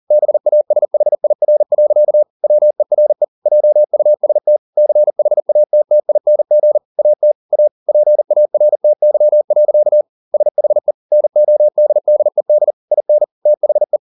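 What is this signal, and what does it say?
Morse code, heard as a keyed tone, about 40 wpm, 600 Hz. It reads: BASHIR. WERE JUST CHATTING AT A PARTY. SHE NODDED IN THE
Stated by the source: Morse code at 40 wpm, practice text, CW med Gnister.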